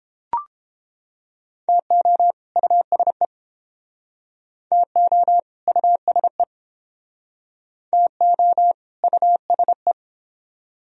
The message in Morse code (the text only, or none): TO USE TO USE TO USE